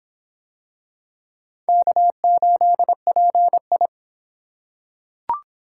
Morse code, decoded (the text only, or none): K8PI